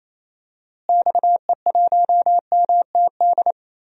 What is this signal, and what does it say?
Morse code, heard as a keyed tone, about 28 wpm, 700 Hz. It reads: XE1MTD
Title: DXCC-100 Call Signs - 6 Characters - Repeated 2x - 28wpm